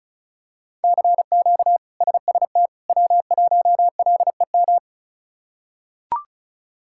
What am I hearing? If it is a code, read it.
Morse code, heard as a keyed tone, about 34 wpm, 700 Hz. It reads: CQ SST W1LEM
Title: K1USN SST QSO Practice - 35wpm